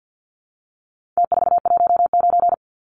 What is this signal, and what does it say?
Morse code, heard as a keyed tone, about 50 wpm, 700 Hz. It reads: T419
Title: Numbers - Contest Style - 4-Digit 50wpm